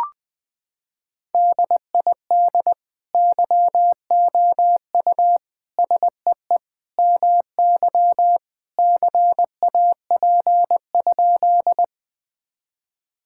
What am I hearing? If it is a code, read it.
Morse code, heard as a keyed tone, about 21 wpm, 700 Hz. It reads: DID YOU SEE MY CAP?
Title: Sentences from Top 1600 Words 20wpm